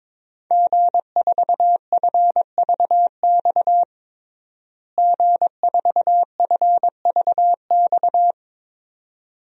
Morse code, heard as a keyed tone, about 22 wpm, 700 Hz. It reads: G4FVX G4FVX
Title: DXCC-100 Call Signs - 5 Characters - Repeated 2x - 22wpm